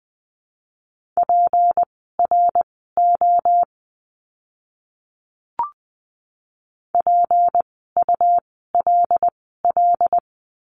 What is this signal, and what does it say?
Morse code, heard as a keyed tone, about 20 wpm, spaced as for 14 wpm, 700 Hz. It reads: PRO PULL
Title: 153 Lesson 25f - Letter p - Words 20wpm